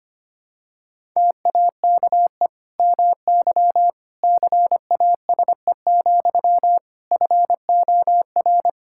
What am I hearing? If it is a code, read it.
Morse code, heard as a keyed tone, about 25 wpm, 700 Hz. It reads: TAKE MY CASE, FOR
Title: Sentences from Top 1000 Words - Review Part 5 - 25wpm